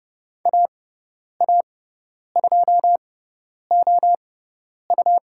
Morse code, heard as a keyed tone, about 30 wpm, spaced as for 10 wpm, 700 Hz. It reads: AA2OU
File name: Lesson 17g - Letter u - Call Signs 30wpm